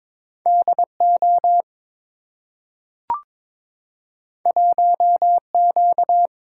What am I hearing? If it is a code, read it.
Morse code, heard as a keyed tone, about 22 wpm, 690 Hz. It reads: DO 1Q
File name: practice / 22wpm / Two Letter-Number Combinations - 22wpm